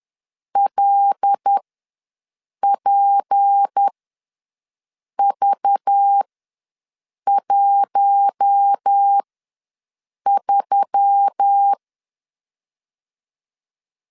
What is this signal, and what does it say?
Morse code, heard as a keyed tone, about 10 wpm, spaced as for 6 wpm, 800 Hz. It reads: LPV13
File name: LEZIONI / 6 PKX0 / 6 P ws 10wpm 3751